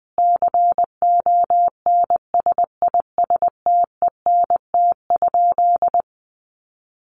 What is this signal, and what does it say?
Morse code, heard as a keyed tone, about 20 wpm, 700 Hz. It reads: CONSISTENT?
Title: Sentences from Top 1500 Words 20wpm